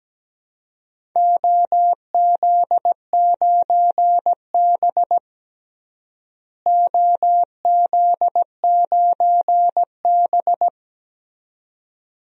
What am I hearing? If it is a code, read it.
Morse code, heard as a keyed tone, about 17 wpm, 700 Hz. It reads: OZ9B OZ9B
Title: DXCC-100 Call Signs - 4 Characters - Repeated 2x - 17wpm